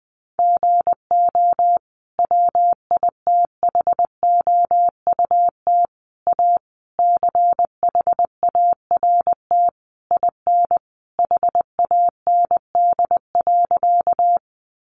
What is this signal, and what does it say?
Morse code, heard as a keyed tone, about 20 wpm, 700 Hz. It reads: GO WITHOUT A CHART IN HAND.
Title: Sentences from Top 1300 Words 20wpm